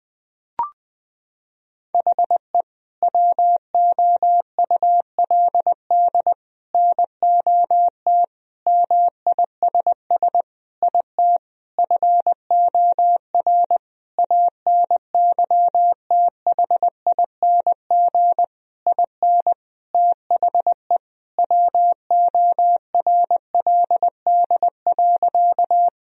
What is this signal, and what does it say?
Morse code, heard as a keyed tone, about 21 wpm, 700 Hz. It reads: HE WOULD NOT MISS IT FOR ANYTHING IN THE WORLD.
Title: Sentences from Top 1000 Words - Review Part 3 - 20wpm